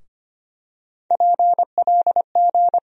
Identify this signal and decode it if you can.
Morse code, heard as a keyed tone, about 25 wpm, 700 Hz. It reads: PLG
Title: USA Call Sign Postfixes - 3 Letters 25wpm